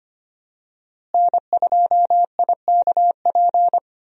Morse code, heard as a keyed tone, about 25 wpm, 700 Hz. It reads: N2IKP
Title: USA Call Signs - 5 Characters - Repeated 3x - 25wpm